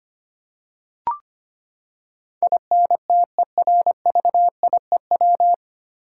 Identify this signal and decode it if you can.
Morse code, heard as a keyed tone, about 25 wpm, 700 Hz. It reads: INTERVIEW
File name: Nine Letter Words - 25wpm